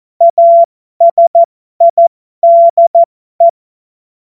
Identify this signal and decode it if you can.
Morse code, heard as a keyed tone, about 14 wpm, spaced as for 12 wpm, 680 Hz. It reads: ASIDE
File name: Morsekurs / Witze